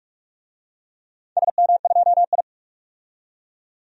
Morse code, heard as a keyed tone, about 44 wpm, 700 Hz. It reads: SM2I
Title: DXCC-100 Call Signs - 4 Characters - 45wpm